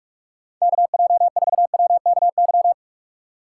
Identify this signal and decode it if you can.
Morse code, heard as a keyed tone, about 59 wpm, 700 Hz. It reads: KJ4WKY